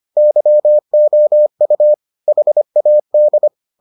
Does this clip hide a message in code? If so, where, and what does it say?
Yes: Morse code from 0.2 to 3.5 s, heard as a keyed tone, about 25 wpm, 600 Hz. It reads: YOU HAD